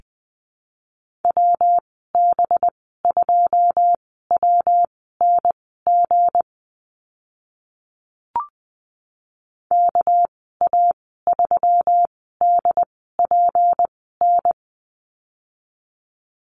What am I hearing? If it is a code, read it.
Morse code, heard as a keyed tone, about 20 wpm, spaced as for 14 wpm, 700 Hz. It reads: WB2WNG KA3DPN
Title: 231 Lesson 37g - Letter x - Call Signs 20wpm